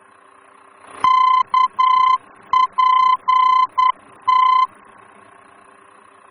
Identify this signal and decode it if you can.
Morse code, heard as a keyed tone, about 10 wpm, 1020 Hz. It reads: KPT
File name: germany / mx / KPT-D-VOR